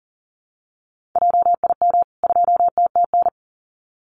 Morse code, heard as a keyed tone, about 40 wpm, 700 Hz. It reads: JIM 2TTN
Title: CWT Exchange 40wpm